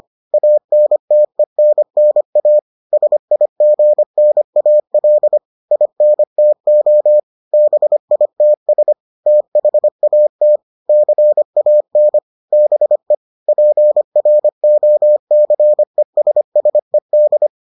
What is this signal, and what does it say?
Morse code, heard as a keyed tone, about 25 wpm, 600 Hz. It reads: ANTENNA SIGNAL INTO BITS THAT CAN BE PROCESSED